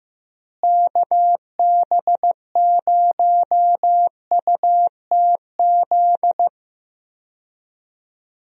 Morse code, heard as a keyed tone, about 15 wpm, 700 Hz. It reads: KB0UTZ
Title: USA Call Signs - 6 Characters - Repeated 3x - 15wpm